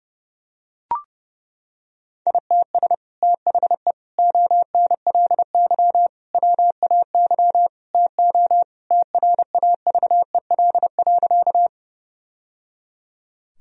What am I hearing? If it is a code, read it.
Morse code, heard as a keyed tone, about 30 wpm, 700 Hz. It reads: ITS THE ONLY WAY TO TRAVEL.
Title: Sentences from Top 1000 Words - Review Part 2 - 30wpm